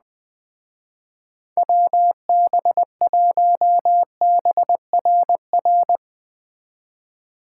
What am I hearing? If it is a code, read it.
Morse code, heard as a keyed tone, about 22 wpm, 700 Hz. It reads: WB1BRR